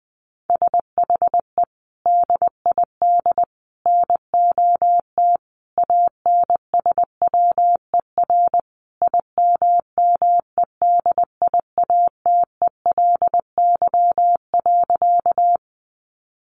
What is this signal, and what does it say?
Morse code, heard as a keyed tone, about 20 wpm, 700 Hz. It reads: SHE DID NOT ANSWER IMMEDIATELY.
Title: Sentences from Top 1100 Words 20wpm